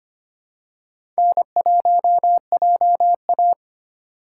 Morse code, heard as a keyed tone, about 25 wpm, 700 Hz. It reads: N1JA
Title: USA Call Signs - 4 Characters - 2x Character Spacing - 25wpm